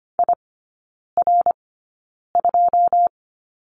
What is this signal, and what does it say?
Morse code, heard as a keyed tone, about 25 wpm, spaced as for 9 wpm, 690 Hz. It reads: IR2
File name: Lesson 14e - Number 2 - Three Characters 25wpm